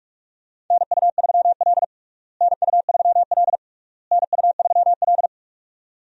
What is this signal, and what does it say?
Morse code, heard as a keyed tone, about 41 wpm, 700 Hz. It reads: NU3L NU3L NU3L
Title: USA Call Signs - 4 Characters - Repeated 3x - 45wpm